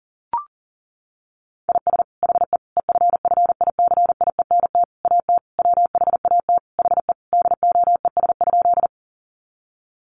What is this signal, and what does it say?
Morse code, heard as a keyed tone, about 37 wpm, 700 Hz. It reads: IS HE EFFICIENT AT WHAT HE DOES?
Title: Sentences from Top 1300 Words 40wpm